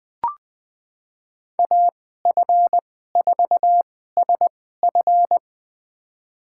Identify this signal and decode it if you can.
Morse code, heard as a keyed tone, about 20 wpm, spaced as for 14 wpm, 700 Hz. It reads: AF4SF
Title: USA Call Signs - 2x Character Spacing - 20wpm